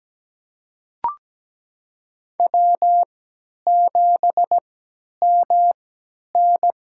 Morse code, heard as a keyed tone, about 17 wpm, spaced as for 10 wpm, 700 Hz. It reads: W7MN
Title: USA Call Signs - 4 Characters - 3x Character Spacing - 17wpm